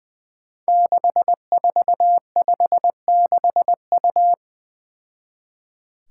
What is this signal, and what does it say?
Morse code, heard as a keyed tone, about 20 wpm, 700 Hz. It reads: 6456U
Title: Five Challenging Letter-Number Combinations 20wpm